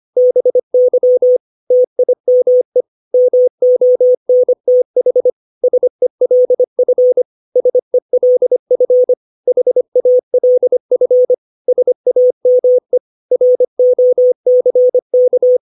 Morse code, heard as a keyed tone, about 25 wpm, 500 Hz. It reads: BY TIME MONTH SELF SELF HALF SAME ROCK
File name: 632 words / 632 Words 25WPM